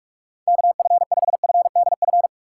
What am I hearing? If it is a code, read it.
Morse code, heard as a keyed tone, about 46 wpm, 700 Hz. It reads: KF5FDF
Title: USA Call Signs - 6 Characters - 6x Character Spacing - 45wpm